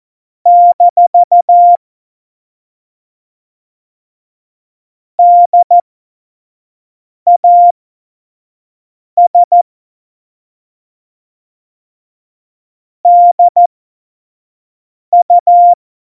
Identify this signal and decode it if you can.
Morse code, heard as a keyed tone, about 14 wpm, spaced as for 5 wpm, 700 Hz. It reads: - DAS DU
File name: Morsekurs / Witze / Witz 17 April 2018, 25 F70